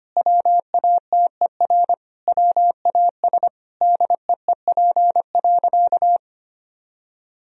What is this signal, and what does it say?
Morse code, heard as a keyed tone, about 25 wpm, 700 Hz. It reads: WATER WAS DEEP.
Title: Sentences from Top 1000 Words - Review Part 3 - 25wpm